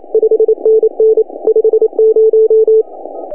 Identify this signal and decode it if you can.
Morse code, heard as a keyed tone, about 28 wpm, 440 Hz. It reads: 5NN 50
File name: CW / HA2OS RHOB 2017 CW